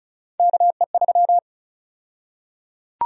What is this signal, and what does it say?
Morse code, heard as a keyed tone, about 34 wpm, 700 Hz. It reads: KE3
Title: USA Call Sign Prefixes - 35wpm